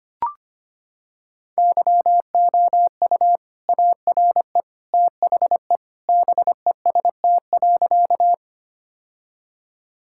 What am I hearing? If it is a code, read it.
Morse code, heard as a keyed tone, about 25 wpm, 700 Hz. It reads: YOU ARE THE BEST.
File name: practice / 25wpm / Sentences with 2-Word Phrases - 25wpm